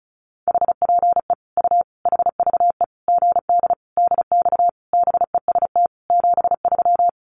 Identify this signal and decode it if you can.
Morse code, heard as a keyed tone, about 34 wpm, 700 Hz. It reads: HPE U HVE GD DX BEST 73
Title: Common QSO Phrases 35wpm